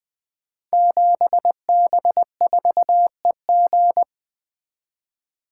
Morse code, heard as a keyed tone, about 20 wpm, 700 Hz. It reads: 7B4EG
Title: Code Groups - 5 Letter:Numbers - 20wpm